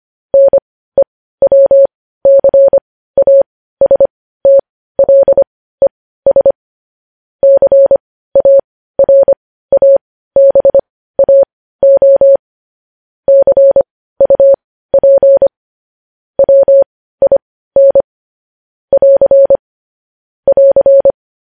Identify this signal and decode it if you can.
Morse code, heard as a keyed tone, about 25 wpm, spaced as for 15 wpm, 550 Hz. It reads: NEWCASTLES CARABAO CUP WIN <AR> <AR>
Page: News Headlines in Morse Code at 15 WPM